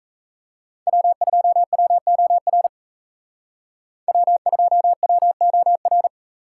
Morse code, heard as a keyed tone, about 33 wpm, 700 Hz. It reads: W2WOR W2WOR
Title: Speed Racing - USA 1x3 Call Signs 28wpm